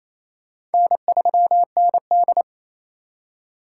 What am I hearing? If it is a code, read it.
Morse code, heard as a keyed tone, about 28 wpm, 700 Hz. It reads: N3ND
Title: K1USN SST QSO Practice - 28wpm